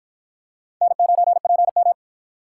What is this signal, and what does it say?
Morse code, heard as a keyed tone, about 51 wpm, 700 Hz. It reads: N9PM